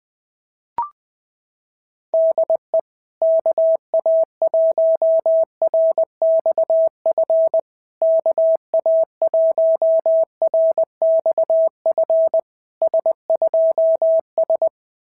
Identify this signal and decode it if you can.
Morse code, heard as a keyed tone, about 20 wpm, 650 Hz. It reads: DE KA1RXF KA1RXF S2S
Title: SOTA QSO Practice 20wpm